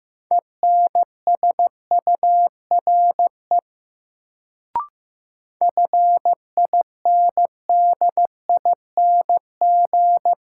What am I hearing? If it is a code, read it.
Morse code, heard as a keyed tone, about 15 wpm, 700 Hz. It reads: ENSURE FINDING
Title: Sentences from Top 1100 Words - New Word Drill - 15wpm